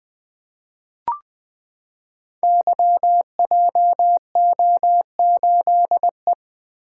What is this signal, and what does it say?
Morse code, heard as a keyed tone, about 20 wpm, 700 Hz. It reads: YJO8E
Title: Code Groups - 5 Letter:Numbers - 20wpm